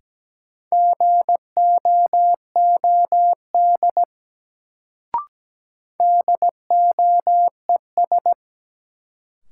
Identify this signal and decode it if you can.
Morse code, heard as a keyed tone, about 17 wpm, 700 Hz. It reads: GOOD DOES
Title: Sets of 2 Words - New Word Drill - 17wpm